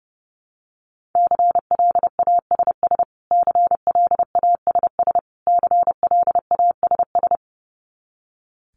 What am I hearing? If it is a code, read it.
Morse code, heard as a keyed tone, about 30 wpm, 700 Hz. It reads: CLASS CLASS CLASS
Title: Top 500 Words Repeated 30wpm